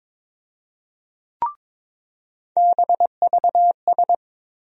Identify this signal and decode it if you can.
Morse code, heard as a keyed tone, about 22 wpm, 700 Hz. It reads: BVS
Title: USA Call Sign Postfixes - 3 Letters 22wpm